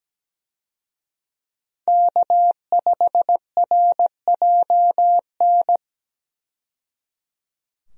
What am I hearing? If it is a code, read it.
Morse code, heard as a keyed tone, about 17 wpm, 700 Hz. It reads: K5RJN
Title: USA Call Signs - 5 Characters - 17wpm